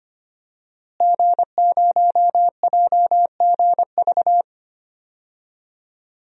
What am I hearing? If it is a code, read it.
Morse code, heard as a keyed tone, about 25 wpm, 700 Hz. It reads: G0JGV
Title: DXCC-100 Call Signs - 5 Characters - Repeated 3x - 25wpm